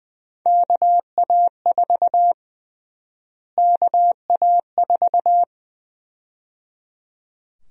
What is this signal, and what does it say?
Morse code, heard as a keyed tone, about 20 wpm, 700 Hz. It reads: KA4 KA4
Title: USA Call Sign Prefixes - Repeated 2x - 20wpm